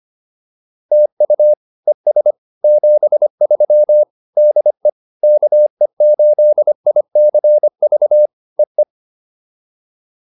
Morse code, heard as a keyed tone, about 25 wpm, 600 Hz. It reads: TU ES 73 DE KE8ICV EE